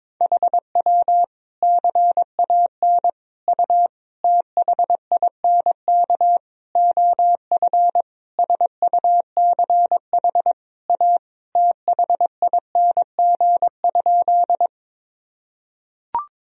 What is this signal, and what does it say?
Morse code, heard as a keyed tone, about 22 wpm, 700 Hz. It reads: HW CAN U THINK OF SUCH A THING?
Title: Sentences from Top 200 Words and Abbreviations 22wpm